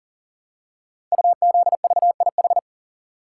Morse code, heard as a keyed tone, about 38 wpm, 700 Hz. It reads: UZVIH
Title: Code Groups - 5 Letters - 40wpm